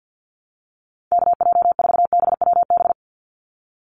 Morse code, heard as a keyed tone, about 50 wpm, 700 Hz. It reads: KJ4BWB
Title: USA Call Signs 50wpm